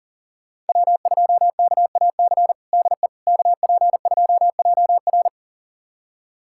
Morse code, heard as a keyed tone, about 37 wpm, 700 Hz. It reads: W2KAC DE KP2JR
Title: USA Callsign DE Callsign - 5 Characters - 40wpm